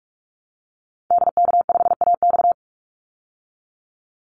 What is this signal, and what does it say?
Morse code, heard as a keyed tone, about 44 wpm, 700 Hz. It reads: DK5AX
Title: Call Signs Repeated 2x - 45wpm